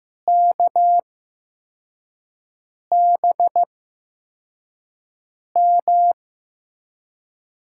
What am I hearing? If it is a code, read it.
Morse code, heard as a keyed tone, about 16 wpm, spaced as for 4 wpm, 700 Hz. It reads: KBM
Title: USA Call Signs - 6 Characters - 8x Character Spacing - 15wpm